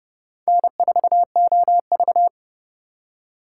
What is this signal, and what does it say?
Morse code, heard as a keyed tone, about 31 wpm, 700 Hz. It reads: N4OV